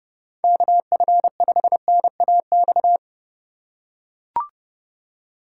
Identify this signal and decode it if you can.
Morse code, heard as a keyed tone, about 31 wpm, 700 Hz. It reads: KF5NAX
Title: USA Call Signs - 4x Character Spacing - 30wpm